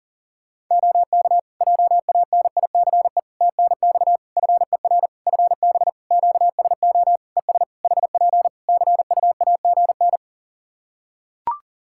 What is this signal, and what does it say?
Morse code, heard as a keyed tone, about 37 wpm, 700 Hz. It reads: OK JANICE TNX FER FB QSO ES HP CUAGN
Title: K1USN SST QSO Practice 40wpm